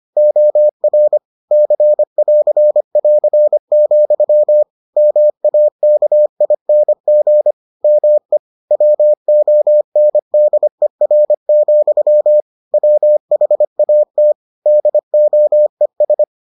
Morse code, heard as a keyed tone, about 25 wpm, 600 Hz. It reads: OR C<AR><AR>, MAKING ME WONDER, WHAT DOES